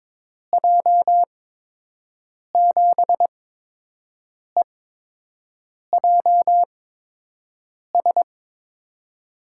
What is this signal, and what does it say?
Morse code, heard as a keyed tone, about 22 wpm, spaced as for 6 wpm, 700 Hz. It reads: J7EJS
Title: USA Call Signs - 6 Characters - 8x Character Spacing - 22wpm